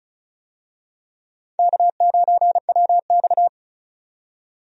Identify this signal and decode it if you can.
Morse code, heard as a keyed tone, about 35 wpm, 700 Hz. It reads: K9WX
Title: USA Call Signs - 4 Characters - 35wpm